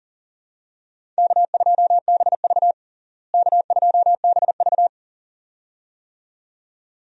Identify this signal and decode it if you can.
Morse code, heard as a keyed tone, about 40 wpm, 700 Hz. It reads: K2BV K2BV
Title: USA Call Signs - 4 Characters - Repeated 2x - 40wpm